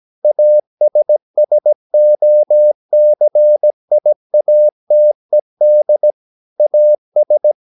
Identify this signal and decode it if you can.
Morse code, heard as a keyed tone, about 17 wpm, 600 Hz. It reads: ASSOCIATED AS
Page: Romanen Never af Ken Follet på engelsk.